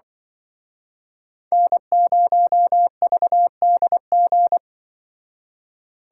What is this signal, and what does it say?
Morse code, heard as a keyed tone, about 24 wpm, 700 Hz. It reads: N0VDG
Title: Speed Racing - USA 1x3 Call Signs 20wpm